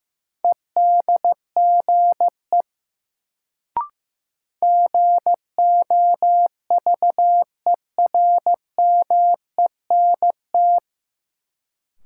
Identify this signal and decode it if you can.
Morse code, heard as a keyed tone, about 16 wpm, 700 Hz. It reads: EDGE GOVERMENT
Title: Sentences from Top 1300 Words - New Word Drill Alt - 15wpm